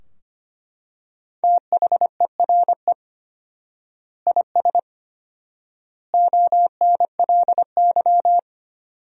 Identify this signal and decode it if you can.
Morse code, heard as a keyed tone, about 25 wpm, 700 Hz. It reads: THERE IS ONLY
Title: 4x Word-Spacing - Sentences from Top 300 Words - 25wpm